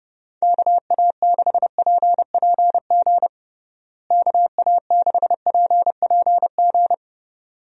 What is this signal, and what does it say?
Morse code, heard as a keyed tone, about 32 wpm, 700 Hz. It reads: KA6PPG KA6PPG
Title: USA Call Signs - 6 Characters - Repeated 2x - 30wpm